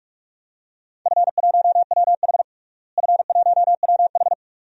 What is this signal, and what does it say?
Morse code, heard as a keyed tone, about 41 wpm, 700 Hz. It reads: F1WH F1WH
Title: DXCC-100 Call Signs - 4 Characters - Repeated 3x - 45wpm